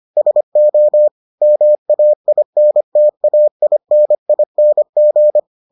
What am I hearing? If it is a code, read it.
Morse code, heard as a keyed tone, about 25 wpm, 600 Hz. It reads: SO MAINTAINING